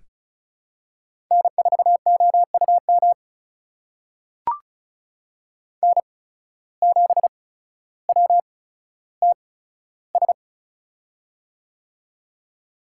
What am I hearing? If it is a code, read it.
Morse code, heard as a keyed tone, about 36 wpm, spaced as for 15 wpm, 700 Hz. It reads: N4OUM N7WTS
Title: USA Call Signs - 5 Characters - 8x Character Spacing - 35wpm